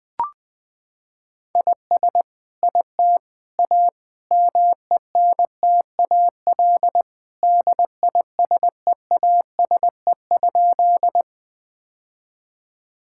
Morse code, heard as a keyed tone, about 20 wpm, 700 Hz. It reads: IS IT A MENTAL DISEASE?
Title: Sentences from Top 1000 Words - Review Part 6 - 20wpm